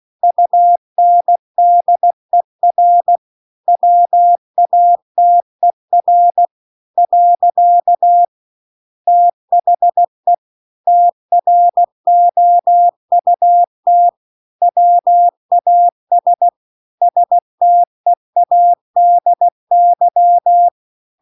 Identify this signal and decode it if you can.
Morse code, heard as a keyed tone, about 16 wpm, 700 Hz. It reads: UNDER WATER . THE TROUT WAS STEADY
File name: CW-The-Big-Twoheaded-River-2-16WPM